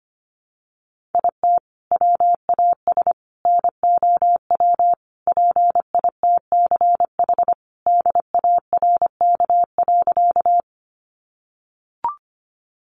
Morse code, heard as a keyed tone, about 25 wpm, 700 Hz. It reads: IT WAS NOW PITCH DARK.
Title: Sentences from Top 1300 Words 25wpm